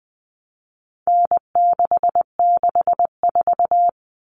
Morse code, heard as a keyed tone, about 20 wpm, 700 Hz. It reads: N664
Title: All Cut Numbers - 4-Digit 20wpm